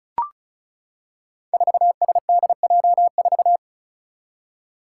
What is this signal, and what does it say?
Morse code, heard as a keyed tone, about 36 wpm, 700 Hz. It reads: 4SDJ4
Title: Five Challenging Letter-Number Combinations 35wpm